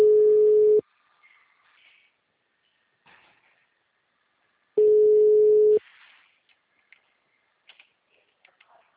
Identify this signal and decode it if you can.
Ringback tone (Europe, 425 Hz).